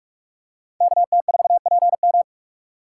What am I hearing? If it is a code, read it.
Morse code, heard as a keyed tone, about 44 wpm, 700 Hz. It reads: KT4PM
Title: USA Call Signs - 5 Characters - 4x Character Spacing - 45wpm